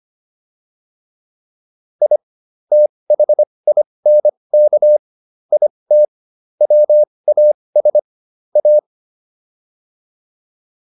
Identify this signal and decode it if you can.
Morse code, heard as a keyed tone, about 25 wpm, spaced as for 20 wpm, 600 Hz. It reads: I THINK IT WAS A